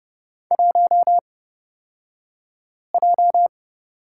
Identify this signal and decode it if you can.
Morse code, heard as a keyed tone, about 30 wpm, spaced as for 5 wpm, 700 Hz. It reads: 1J
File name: Lesson 37d - Letter x - Two Characters 30wpm